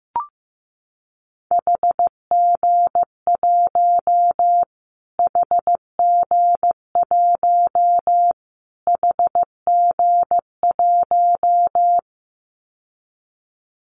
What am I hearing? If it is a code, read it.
Morse code, heard as a keyed tone, about 15 wpm, 700 Hz. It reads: HG1 HG1 HG1
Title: Callsign Prefixes for Top 25 DXCC - Repeated 3x - 15wpm